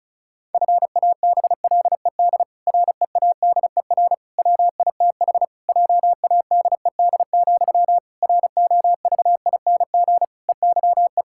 Morse code, heard as a keyed tone, about 34 wpm, 700 Hz. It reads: FABLED READER WITH JADED, ROVING EYE
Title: Pangram Challenge v2 - 35wpm